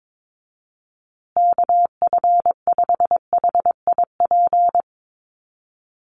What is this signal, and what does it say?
Morse code, heard as a keyed tone, about 22 wpm, 700 Hz. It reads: KF5HIP
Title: USA Call Signs Repeated 3x 22wpm